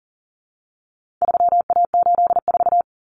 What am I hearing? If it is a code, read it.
Morse code, heard as a keyed tone, about 40 wpm, 700 Hz. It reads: 3A84
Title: Common Cut Numbers - 4-Digit 40wpm